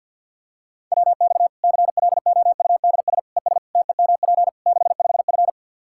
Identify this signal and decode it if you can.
Morse code, heard as a keyed tone, about 46 wpm, 700 Hz. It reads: WX CLOUDS ES TEMP 65F
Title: Practice QSO - Exchange 2 - Info - 50wpm